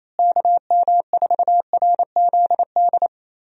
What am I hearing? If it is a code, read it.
Morse code, heard as a keyed tone, about 28 wpm, 700 Hz. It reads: KM4RZD
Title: USA Call Signs - 6 Characters - Repeated 2x - 28wpm